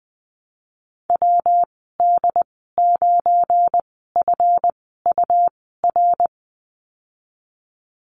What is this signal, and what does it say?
Morse code, heard as a keyed tone, about 20 wpm, spaced as for 15 wpm, 700 Hz. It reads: WD9FUR
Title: 173 Lesson 28g - Number 9 - Call Signs 20wpm